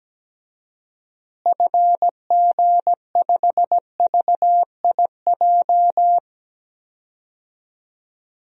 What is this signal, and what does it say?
Morse code, heard as a keyed tone, about 17 wpm, 700 Hz. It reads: FG5VIJ
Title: DXCC-100 Call Signs - 6 Characters - Repeated 2x - 17wpm